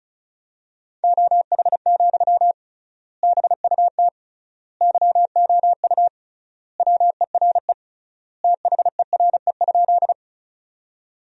Morse code, heard as a keyed tone, about 34 wpm, 700 Hz. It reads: OH, BUT YOU WERE THERE?
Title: 3x Word-Spacing - Sentences from Top 500 Words - 35wpm